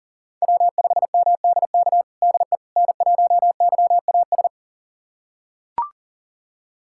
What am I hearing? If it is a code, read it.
Morse code, heard as a keyed tone, about 40 wpm, 700 Hz. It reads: W5MDK DE N1YAS
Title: USA Callsign DE Callsign - 5 Characters - 40wpm